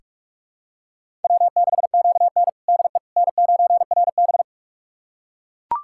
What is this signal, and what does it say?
Morse code, heard as a keyed tone, about 41 wpm, 700 Hz. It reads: W6QN DE N9RB
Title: USA Callsign DE Callsign - 4 Characters - 45wpm